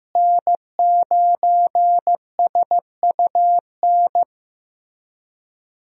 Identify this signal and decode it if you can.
Morse code, heard as a keyed tone, about 15 wpm, 700 Hz. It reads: N9SUN
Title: USA Call Signs - 5 Characters - Repeated 3x - 15wpm